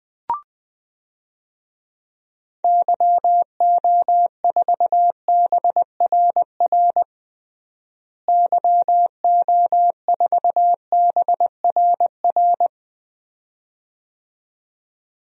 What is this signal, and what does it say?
Morse code, heard as a keyed tone, about 20 wpm, 700 Hz. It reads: YO4BRR YO4BRR
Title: DXCC-100 Call Signs - 6 Characters - Repeated 2x - 20wpm